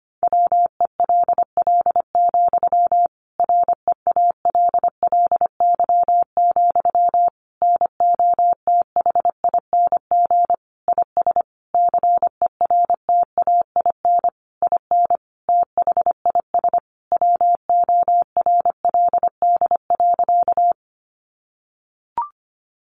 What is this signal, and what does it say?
Morse code, heard as a keyed tone, about 25 wpm, 700 Hz. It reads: WELL, REALLY, NOTHING IS CERTAIN IN THIS WORLD.
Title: Sentences from Top 600 Words 25wpm